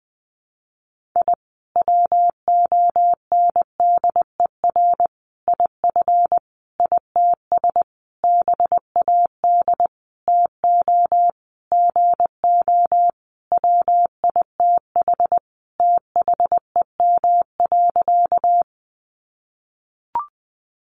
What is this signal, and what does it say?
Morse code, heard as a keyed tone, about 20 wpm, 700 Hz. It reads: I WONDER IF ITS BAD TO GO WITH THEM.
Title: Sentences from Top 700 Words 20wpm